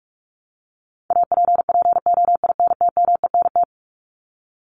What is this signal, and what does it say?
Morse code, heard as a keyed tone, about 44 wpm, 700 Hz. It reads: APPOINTMENT
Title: Top 1751-2000 Words 45wpm